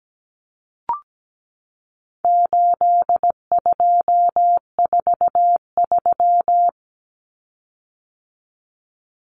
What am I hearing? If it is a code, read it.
Morse code, heard as a keyed tone, about 17 wpm, 700 Hz. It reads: 8243
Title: Common Cut Numbers - 4-Digit 17wpm